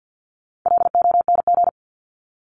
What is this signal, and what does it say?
Morse code, heard as a keyed tone, about 50 wpm, 700 Hz. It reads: LONG
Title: Top 300 Words Repeated 50wpm